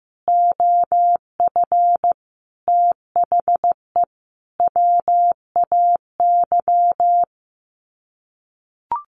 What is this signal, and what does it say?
Morse code, heard as a keyed tone, about 15 wpm, 700 Hz. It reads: OF THE WAY
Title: Sets of 5 Words Encore 15wpm